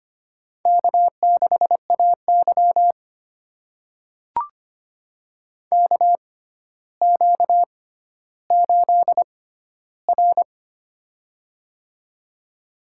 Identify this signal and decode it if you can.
Morse code, heard as a keyed tone, about 25 wpm, spaced as for 13 wpm, 700 Hz. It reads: K6AY KQ8R
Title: USA Call Signs - 4 Characters - 6x Character Spacing - 25wpm